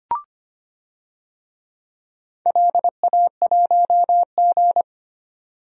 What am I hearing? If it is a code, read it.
Morse code, heard as a keyed tone, about 25 wpm, 700 Hz. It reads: LA1G
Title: DXCC-100 Call Signs - 4 Characters - 25wpm